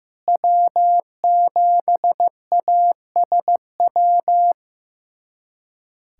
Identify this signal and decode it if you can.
Morse code, heard as a keyed tone, about 15 wpm, 700 Hz. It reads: W7ASW
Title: USA Callsign DE Callsign - 5 Characters - 15wpm